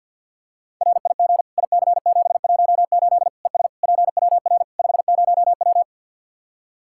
Morse code, heard as a keyed tone, about 47 wpm, 700 Hz. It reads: RIG IC718 ES PWR 50W
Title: Practice QSO - Exchange 2 - Info - 50wpm